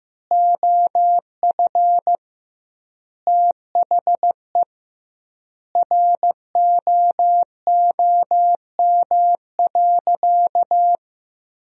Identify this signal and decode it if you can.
Morse code, heard as a keyed tone, about 15 wpm, 700 Hz. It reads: OF THE ROOM.
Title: 2x Word-Spacing - Sentences from Top 500 Words - 15wpm